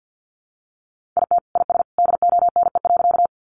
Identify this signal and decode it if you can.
Morse code, heard as a keyed tone, about 50 wpm, 700 Hz. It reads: IT IS DONE.
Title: Sentences from Top 500 Words 50wpm